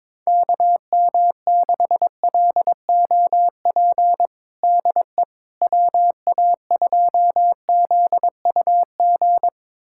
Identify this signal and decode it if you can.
Morse code, heard as a keyed tone, about 21 wpm, 700 Hz. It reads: KM6LOP DE WA2ZUG
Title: USA Callsign DE Callsign - 6 Characters - 22wpm